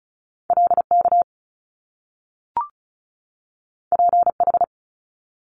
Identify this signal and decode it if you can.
Morse code, heard as a keyed tone, about 36 wpm, 700 Hz. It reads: LK PH
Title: Top Two Letter Combinations in Words 35wpm